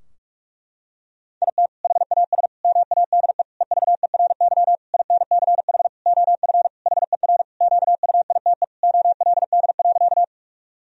Morse code, heard as a keyed tone, about 41 wpm, 700 Hz. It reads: IT HAS MADE EVERY INCH OF HER QUITE OLD.
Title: Sentences from Top 1000 Words - Review Part 3 - 45wpm